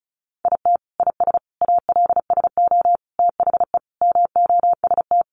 Morse code, heard as a keyed tone, about 34 wpm, 700 Hz. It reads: IT IS ALSO THE MOST
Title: Sentences from Top 1400 Words 35wpm